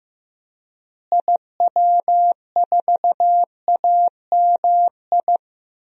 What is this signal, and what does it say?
Morse code, heard as a keyed tone, about 15 wpm, 700 Hz. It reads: IW4AMI
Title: DXCC-100 Call Signs - 6 Characters - Repeated 3x - 15wpm